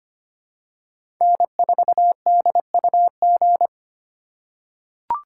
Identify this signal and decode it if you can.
Morse code, heard as a keyed tone, about 25 wpm, 700 Hz. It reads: N4DUG
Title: USA Call Signs - 5 Characters - 2x Character Spacing - 25wpm